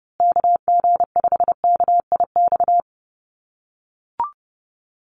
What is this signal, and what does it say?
Morse code, heard as a keyed tone, about 30 wpm, 700 Hz. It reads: KG5KIX
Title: USA Call Signs Repeated 2x 30wpm